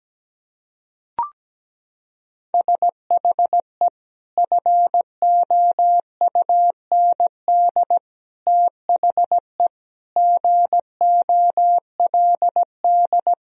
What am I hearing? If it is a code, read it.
Morse code, heard as a keyed tone, about 17 wpm, 700 Hz. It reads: SHE FOUND THE GOLD
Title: Sentences from Top 1000 Words - Review Part 3 - 17wpm